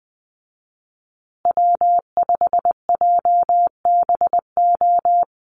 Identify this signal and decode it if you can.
Morse code, heard as a keyed tone, about 20 wpm, 700 Hz. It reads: W5JBO
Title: Call Signs Repeated 2x - 20wpm